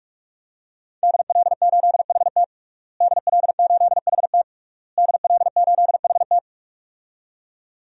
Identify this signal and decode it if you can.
Morse code, heard as a keyed tone, about 47 wpm, 700 Hz. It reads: DL8HT DL8HT DL8HT
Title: DXCC-100 Call Signs - 5 Characters - Repeated 3x - 45wpm